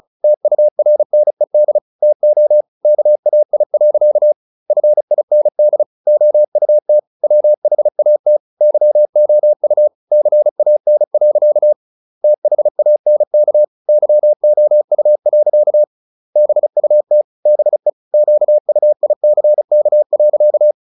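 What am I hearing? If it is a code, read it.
Morse code, heard as a keyed tone, about 35 wpm, 600 Hz. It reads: TURNED TO KAI. FIND OUT WHAT YOU CAN. THANK YOU. BUT BE QUICK.
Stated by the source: Morse practice at 35 wpm from CW med Gnister, sending Ken Follett's novel Never in English.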